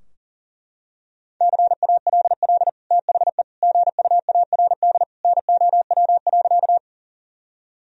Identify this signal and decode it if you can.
Morse code, heard as a keyed tone, about 40 wpm, 700 Hz. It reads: CALL THE GUARD NOW.
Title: Sentences from Top 1000 Words - Review Part 4 - 40wpm